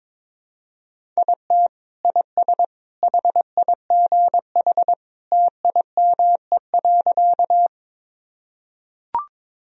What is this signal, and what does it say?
Morse code, heard as a keyed tone, about 21 wpm, 700 Hz. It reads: IT IS HIGH TIME.
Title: Sentences from Top 1000 Words - Review Part 1 - 22wpm